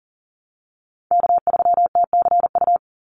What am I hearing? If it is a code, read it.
Morse code, heard as a keyed tone, about 40 wpm, 700 Hz. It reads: K3TCU
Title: NA SOTA Chaser Callsigns 40wpm